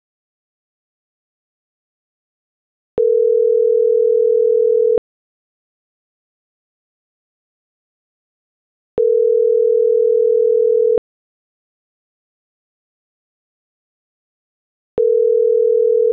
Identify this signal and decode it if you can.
Ringback tone (North America), 440 + 480 Hz.